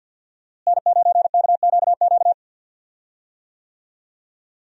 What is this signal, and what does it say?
Morse code, heard as a keyed tone, about 46 wpm, 700 Hz. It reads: N9KQQ
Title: USA Call Signs - 5 Characters - Repeated 2x - 50wpm